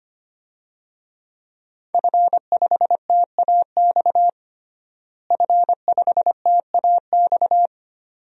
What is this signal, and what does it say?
Morse code, heard as a keyed tone, about 25 wpm, 700 Hz. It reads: F5TAX F5TAX
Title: DXCC-100 Call Signs - 5 Characters - Repeated 2x - 25wpm